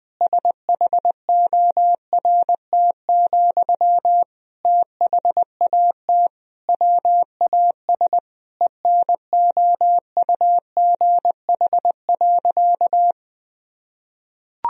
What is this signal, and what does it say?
Morse code, heard as a keyed tone, about 20 wpm, 700 Hz. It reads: SHORT, THAT WAS ENOUGH.
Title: Sentences from Top 1000 Words - Review Part 2 - 20wpm